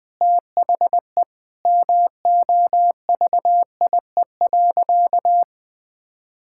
Morse code, heard as a keyed tone, about 20 wpm, 700 Hz. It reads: THE MOVIE.
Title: Sentences with 4-Word Phrases - 20wpm